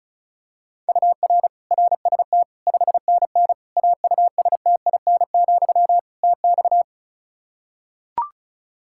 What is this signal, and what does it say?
Morse code, heard as a keyed tone, about 36 wpm, 700 Hz. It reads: UR RST 5NN AUSTIN, TX
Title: Practice QSO - Exchange 1 - Beginning - 35wpm